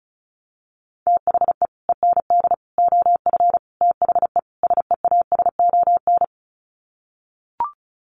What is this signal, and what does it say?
Morse code, heard as a keyed tone, about 34 wpm, 700 Hz. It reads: THE END OF THE SEASON
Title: Sets of 5 Words Encore 35wpm